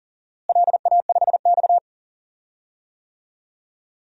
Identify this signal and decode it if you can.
Morse code, heard as a keyed tone, about 40 wpm, 700 Hz. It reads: LA5X
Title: DXCC-100 Call Signs - 4 Characters - Repeated 3x - 40wpm